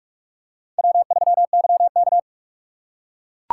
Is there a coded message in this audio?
Yes: Morse code from 0.8 to 2.2 s, heard as a keyed tone, about 44 wpm, 700 Hz. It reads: W3YK